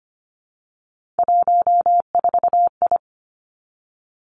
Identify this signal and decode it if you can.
Morse code, heard as a keyed tone, about 25 wpm, 700 Hz. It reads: 14I
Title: Three Challenging Letter-Number Combinations 25wpm